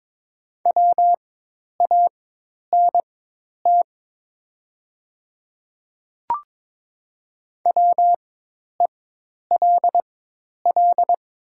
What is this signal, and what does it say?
Morse code, heard as a keyed tone, about 22 wpm, spaced as for 10 wpm, 700 Hz. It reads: WANT WELL
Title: 4x Character Spacing - Top 200 Words - 22wpm